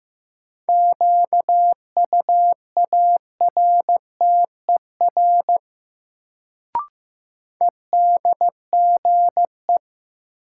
Morse code, heard as a keyed tone, about 15 wpm, 700 Hz. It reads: QUARTER EDGE
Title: Sentences from Top 1300 Words - New Word Drill Alt - 15wpm